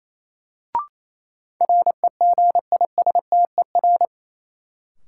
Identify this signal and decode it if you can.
Morse code, heard as a keyed tone, about 28 wpm, 700 Hz. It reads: REGISTER
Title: Sentences from Top 1200 Words - New Word Drill - 28wpm